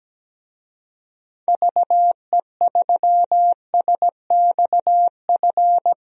Morse code, heard as a keyed tone, about 17 wpm, 700 Hz. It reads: VE3SXF
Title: DXCC-100 Call Signs - 6 Characters - Repeated 2x - 17wpm